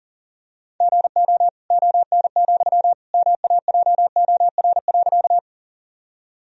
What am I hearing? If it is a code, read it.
Morse code, heard as a keyed tone, about 40 wpm, 700 Hz. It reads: GO ON, MAJOR.
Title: Sentences from Top 1000 Words - Review Part 4 - 40wpm